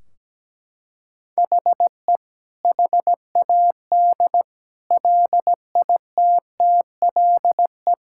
Morse code, heard as a keyed tone, about 17 wpm, 700 Hz. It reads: HE HAD LITTLE
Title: Sentences from Top 1600 Words 17wpm